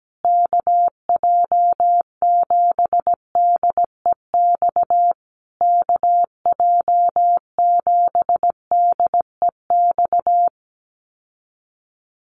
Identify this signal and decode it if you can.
Morse code, heard as a keyed tone, about 17 wpm, 700 Hz. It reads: KJ7DEX KJ7DEX
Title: USA Call Signs Repeated 3x 17wpm